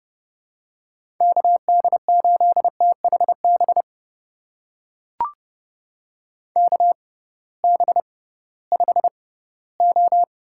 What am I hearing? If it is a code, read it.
Morse code, heard as a keyed tone, about 30 wpm, spaced as for 18 wpm, 700 Hz. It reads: KD8THB KB5O
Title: USA Call Signs - 6 Characters - 6x Character Spacing - 30wpm